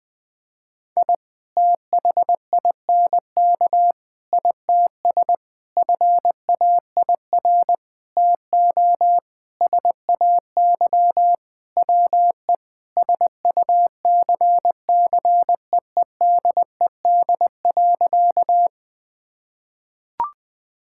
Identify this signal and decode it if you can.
Morse code, heard as a keyed tone, about 20 wpm, 700 Hz. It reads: I THINK ITS FAIR TO SAY WE SUCCEEDED.
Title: Sentences with 6-Word Phrases - 20wpm